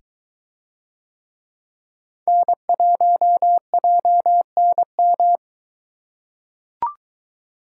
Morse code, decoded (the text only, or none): N1JNM